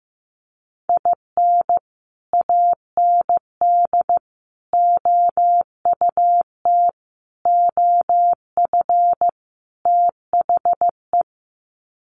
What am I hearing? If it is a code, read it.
Morse code, heard as a keyed tone, about 15 wpm, 700 Hz. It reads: IN AND OUT OF THE
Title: Sets of 5 Words 15wpm